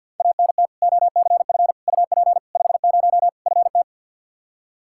Morse code, heard as a keyed tone, about 50 wpm, 700 Hz. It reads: ANT OCF UP 50 FT
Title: Practice QSO - Exchange 2 - Info - 50wpm